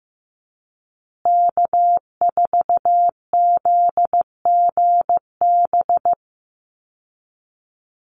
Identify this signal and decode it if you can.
Morse code, heard as a keyed tone, about 15 wpm, 700 Hz. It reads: K4ZGB
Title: Call Signs Repeated 2x - 15wpm